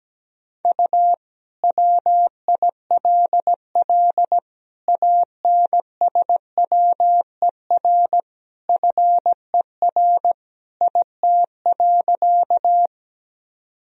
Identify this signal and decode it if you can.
Morse code, heard as a keyed tone, about 17 wpm, 700 Hz. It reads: U WILL ANSWER FER IT.
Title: Sentences from Top 300 Words and Abbreviations 17wpm